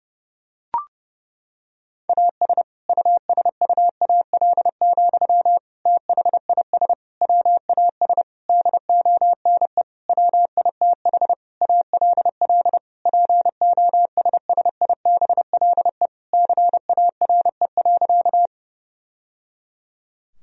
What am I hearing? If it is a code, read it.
Morse code, heard as a keyed tone, about 29 wpm, 700 Hz. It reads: AS USUAL, THIS WAS DONE WITH ALL POSSIBLE CARE.
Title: Sentences from Top 1000 Words - Review Part 2 - 30wpm